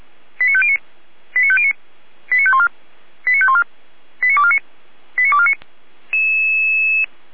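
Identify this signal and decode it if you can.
Five-tone selective call (ZVEI), heard as quick runs of five tones, tones of about 70 ms. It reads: 89509 89509 89514 89514 89249 89249
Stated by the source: ZVEI five-tone selective call received with an SDR, from fire brigade Feuerwehr Orlamünde.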